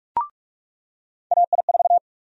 Morse code, heard as a keyed tone, about 46 wpm, 700 Hz. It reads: AI4
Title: USA Call Sign Prefixes - 45wpm